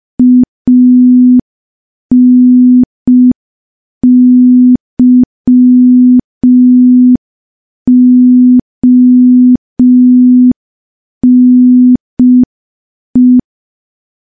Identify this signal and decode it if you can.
Morse code, heard as a keyed tone, about 5 wpm, 250 Hz. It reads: ANYONE
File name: morse / morse-training-05WPM